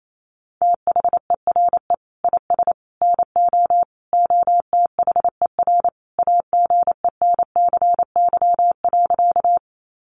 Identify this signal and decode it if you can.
Morse code, heard as a keyed tone, about 28 wpm, 700 Hz. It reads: THERE IS NO OTHER AGENCY.
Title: Sentences from Top 1300 Words 28wpm